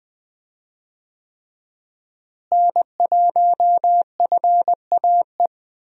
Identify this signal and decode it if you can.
Morse code, heard as a keyed tone, about 21 wpm, 700 Hz. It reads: N1FAE
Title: USA Call Signs - 5 Characters - Repeated 3x - 20wpm